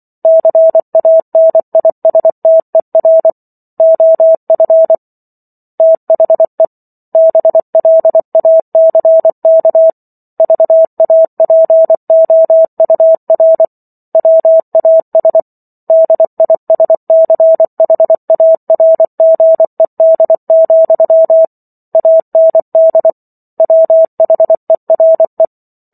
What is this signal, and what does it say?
Morse code, heard as a keyed tone, about 24 wpm, 650 Hz. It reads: CANISTER OF THE BLACK VAPOUR WAS DISCHARGED, AND WHERE
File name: Chapter 15 - 24 WPM